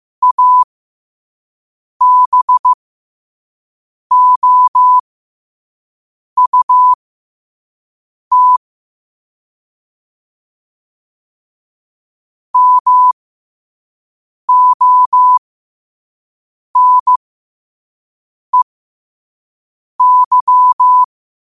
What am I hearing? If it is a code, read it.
Morse code, heard as a keyed tone, about 15 wpm, spaced as for 5 wpm, 1000 Hz. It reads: ABOUT MONEY
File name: qotdfiles / 5wpm